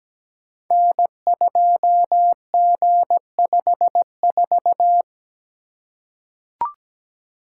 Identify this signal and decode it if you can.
Morse code, heard as a keyed tone, about 17 wpm, 700 Hz. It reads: N2G54
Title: Code Groups - 5 Letter:Numbers - 17wpm